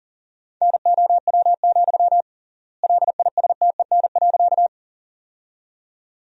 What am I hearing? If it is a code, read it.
Morse code, heard as a keyed tone, about 40 wpm, 700 Hz. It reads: NOW, LISTEN.
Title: 3x Word-Spacing - Sentences from Top 400 Words - 40wpm